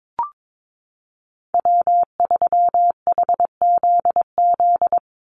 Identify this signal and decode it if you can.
Morse code, heard as a keyed tone, about 22 wpm, 700 Hz. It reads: W3HZZ
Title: Call Signs Repeated 3x - 22wpm